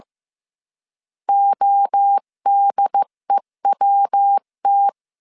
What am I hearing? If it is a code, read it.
Morse code, heard as a keyed tone, about 15 wpm, 800 Hz. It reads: ODEWT
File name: CW / DET / 1 3 1500C S C 14wpm 2458